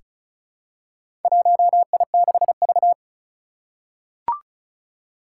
Morse code, heard as a keyed tone, about 36 wpm, 700 Hz. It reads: 1I6V
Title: Four Challenging Letter-Number Combinations 35wpm